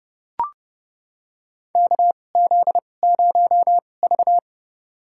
Morse code, heard as a keyed tone, about 31 wpm, spaced as for 22 wpm, 700 Hz. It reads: KZ0V
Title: USA Call Signs - 4 Characters - 2x Character Spacing - 30wpm